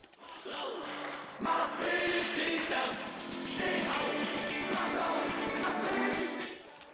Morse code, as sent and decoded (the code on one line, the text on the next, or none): none